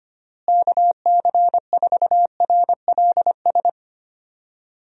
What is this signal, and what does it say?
Morse code, heard as a keyed tone, about 25 wpm, 700 Hz. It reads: KC4RLS
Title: USA Call Signs - 6 Characters - Repeated 2x - 25wpm